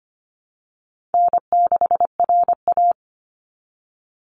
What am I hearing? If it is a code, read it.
Morse code, heard as a keyed tone, about 25 wpm, 700 Hz. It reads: N6RA
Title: Easy Call Signs - 25wpm